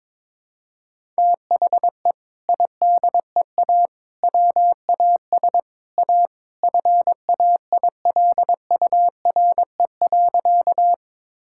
Morse code, heard as a keyed tone, about 22 wpm, 700 Hz. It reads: THE IDEA WAS A FAILURE.
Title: Sentences from Top 1000 Words - Review Part 6 - 22wpm